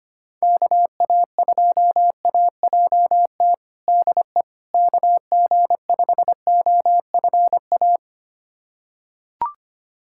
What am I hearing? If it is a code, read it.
Morse code, heard as a keyed tone, about 25 wpm, 700 Hz. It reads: KA2AJT DE KG5OFA
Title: USA Callsign DE Callsign - 6 Characters - 25wpm